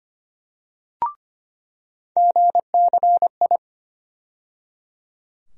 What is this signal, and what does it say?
Morse code, heard as a keyed tone, about 25 wpm, 700 Hz. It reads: GCI
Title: USA Call Sign Postfixes - 3 Letters 25wpm